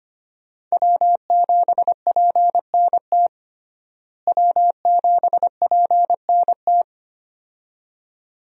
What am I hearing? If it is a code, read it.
Morse code, heard as a keyed tone, about 25 wpm, 700 Hz. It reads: W7PNT W7PNT
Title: USA Call Signs - 5 Characters - Repeated 2x - 25wpm